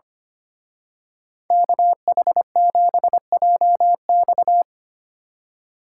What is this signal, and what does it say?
Morse code, heard as a keyed tone, about 26 wpm, 700 Hz. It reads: KH7JX
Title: Speed Racing - USA 2x2 Call Signs 25wpm